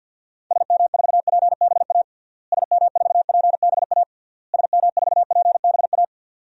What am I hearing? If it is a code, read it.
Morse code, heard as a keyed tone, about 50 wpm, 700 Hz. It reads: SM4PBA SM4PBA SM4PBA
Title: DXCC-100 Call Signs - 6 Characters - Repeated 3x - 50wpm